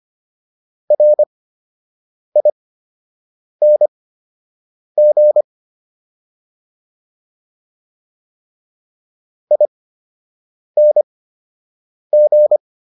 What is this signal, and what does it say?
Morse code, heard as a keyed tone, about 25 wpm, spaced as for 7 wpm, 600 Hz. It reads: RING ING